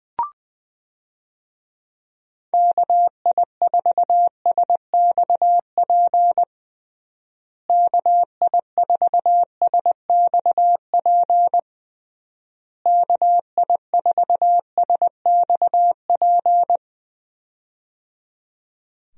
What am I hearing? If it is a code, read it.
Morse code, heard as a keyed tone, about 21 wpm, 700 Hz. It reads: KI4SXP KI4SXP KI4SXP
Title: USA Call Signs - 6 Characters - Repeated 3x - 20wpm